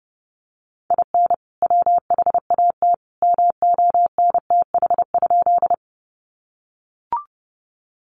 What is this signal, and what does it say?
Morse code, heard as a keyed tone, about 30 wpm, 700 Hz. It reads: IN WHAT MONTH?
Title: Sentences from Top 800 Words 30wpm